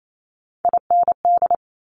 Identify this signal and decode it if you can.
Morse code, heard as a keyed tone, about 28 wpm, 700 Hz. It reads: IND
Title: Multi-letter Phonemes Quick Fire 28wpm